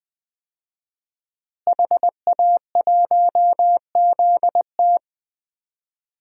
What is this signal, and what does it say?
Morse code, heard as a keyed tone, about 20 wpm, 700 Hz. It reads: HA1ZT
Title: DXCC-100 Call Signs - 5 Characters - Repeated 2x - 20wpm